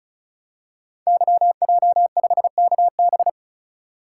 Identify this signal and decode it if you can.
Morse code, heard as a keyed tone, about 36 wpm, 700 Hz. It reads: YJ5KB
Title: Code Groups - 5 Letter:Numbers - 35wpm